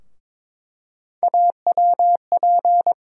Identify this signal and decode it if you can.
Morse code, heard as a keyed tone, about 22 wpm, 700 Hz. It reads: AWP
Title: USA Call Sign Postfixes - 3 Letters 22wpm